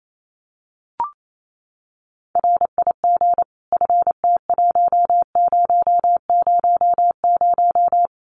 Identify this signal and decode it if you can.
Morse code, heard as a keyed tone, about 28 wpm, 700 Hz. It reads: RIG FT1000
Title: Common QSO Phrases 28wpm